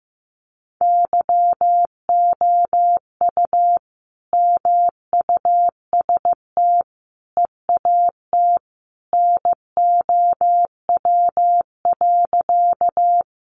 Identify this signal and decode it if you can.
Morse code, heard as a keyed tone, about 15 wpm, 700 Hz. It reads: YOU MUST EAT NOW.
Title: Sentences from Top 300 Words 15wpm